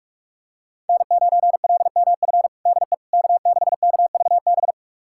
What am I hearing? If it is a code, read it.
Morse code, heard as a keyed tone, about 43 wpm, 700 Hz. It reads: N9LMF DE K6KVB
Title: USA Callsign DE Callsign - 5 Characters - 45wpm